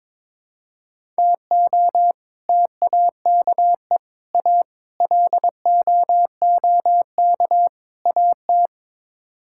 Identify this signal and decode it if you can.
Morse code, heard as a keyed tone, about 21 wpm, 700 Hz. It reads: TO TAKE A LOOK AT
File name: practice / 22wpm / Build A Phrase - 5-Word Phrases - 22wpm